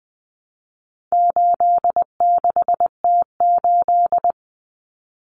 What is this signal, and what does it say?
Morse code, heard as a keyed tone, about 20 wpm, 700 Hz. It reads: 86T8
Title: Common Cut Numbers - 4-Digit 20wpm